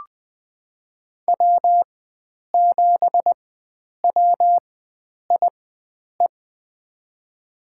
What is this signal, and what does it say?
Morse code, heard as a keyed tone, about 20 wpm, spaced as for 9 wpm, 700 Hz. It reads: W7WIE
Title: USA Call Signs - 4x Character Spacing - 20wpm